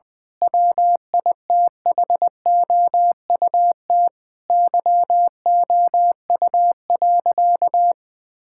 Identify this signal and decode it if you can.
Morse code, heard as a keyed tone, about 20 wpm, 700 Hz. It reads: WITHOUT YOU.
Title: Difficult Sentences with Hints - 20wpm